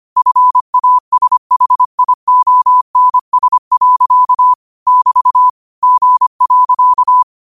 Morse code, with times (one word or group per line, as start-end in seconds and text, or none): 0.2-4.5 FASHIONS.
4.9-5.5 <BT>
5.8-7.2 G.